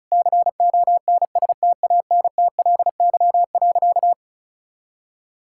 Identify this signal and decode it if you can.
Morse code, heard as a keyed tone, about 36 wpm, 700 Hz. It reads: CONSTANTLY.
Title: Sentences from Top 1600 Words 35wpm